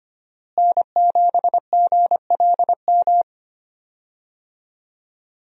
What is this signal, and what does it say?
Morse code, heard as a keyed tone, about 25 wpm, 700 Hz. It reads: N7GLM
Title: USA Call Signs - 5 Characters - Repeated 2x - 25wpm